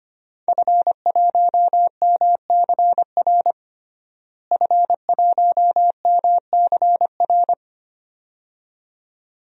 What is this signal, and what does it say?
Morse code, heard as a keyed tone, about 25 wpm, 700 Hz. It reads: F1MCR F1MCR
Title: DXCC-100 Call Signs - 5 Characters - Repeated 2x - 25wpm